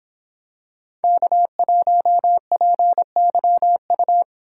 Morse code, heard as a keyed tone, about 25 wpm, 700 Hz. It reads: K1PYU